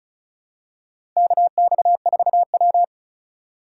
Morse code, heard as a keyed tone, about 34 wpm, 700 Hz. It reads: KX4W
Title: USA Call Signs - 4 Characters - 2x Character Spacing - 35wpm